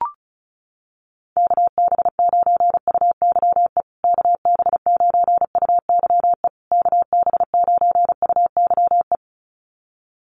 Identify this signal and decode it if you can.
Morse code, heard as a keyed tone, about 34 wpm, 700 Hz. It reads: KB9UYE KB9UYE KB9UYE
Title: USA Call Signs Repeated 3x 35wpm